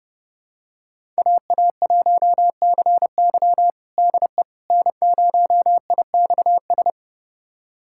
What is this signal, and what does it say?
Morse code, heard as a keyed tone, about 31 wpm, 700 Hz. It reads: AA1CY DE N0IXS
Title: USA Callsign DE Callsign - 5 Characters - 30wpm